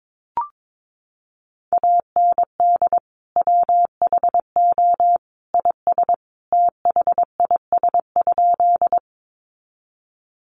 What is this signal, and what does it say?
Morse code, heard as a keyed tone, about 22 wpm, 700 Hz. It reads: AND WHO IS THIS?
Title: Sentences from Top 100 Words 22wpm